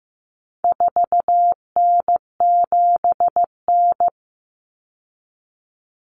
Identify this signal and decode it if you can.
Morse code, heard as a keyed tone, about 15 wpm, 700 Hz. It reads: 4N7N
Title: Common Cut Numbers - 4-Digit 15wpm